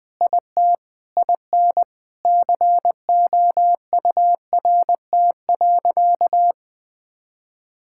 Morse code, heard as a keyed tone, about 20 wpm, 700 Hz. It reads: IT IN COURT.
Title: Sentences from Top 1600 Words 20wpm